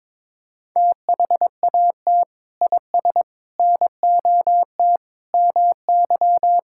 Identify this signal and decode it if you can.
Morse code, heard as a keyed tone, about 22 wpm, 700 Hz. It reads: THAT IS NOT MY
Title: Sentences from Top 1000 Words - Review Part 6 - 22wpm